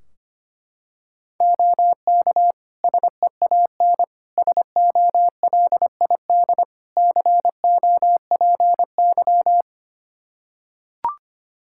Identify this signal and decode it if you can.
Morse code, heard as a keyed tone, about 25 wpm, 700 Hz. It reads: OK SEAN SOLID COPY
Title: Practice QSO - Exchange 3 - More Info - 25wpm